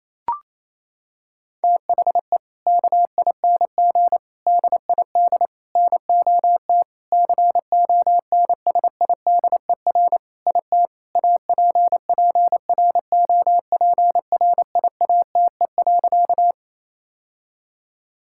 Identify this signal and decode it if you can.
Morse code, heard as a keyed tone, about 28 wpm, 700 Hz. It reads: THE KING DID NOT CONSIDER IT APPROPRIATE.
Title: Sentences from Top 1000 Words - Review Part 5 - 28wpm